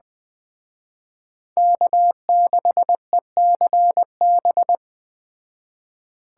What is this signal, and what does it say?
Morse code, heard as a keyed tone, about 20 wpm, 700 Hz. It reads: K6ECB